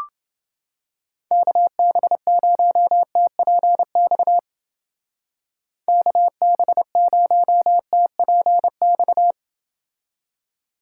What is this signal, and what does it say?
Morse code, heard as a keyed tone, about 28 wpm, 700 Hz. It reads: KB0TPX KB0TPX